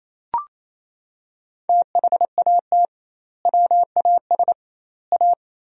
Morse code, heard as a keyed tone, about 28 wpm, 700 Hz. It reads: THAT WAS A
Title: 2x Word-Spacing - Sentences from Top 200 Words - 28wpm